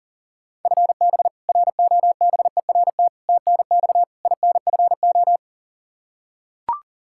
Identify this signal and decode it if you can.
Morse code, heard as a keyed tone, about 40 wpm, 700 Hz. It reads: FB ROBERT TNX INFO
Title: K1USN SST QSO Practice 40wpm